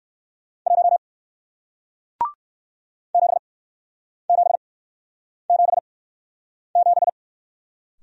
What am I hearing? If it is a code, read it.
Morse code, heard as a keyed tone, about 60 wpm, spaced as for 9 wpm, 700 Hz. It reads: 1 7777